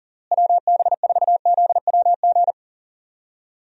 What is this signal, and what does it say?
Morse code, heard as a keyed tone, about 40 wpm, 700 Hz. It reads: WB4ZWG
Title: USA Call Signs - 6 Characters - Repeated 3x - 40wpm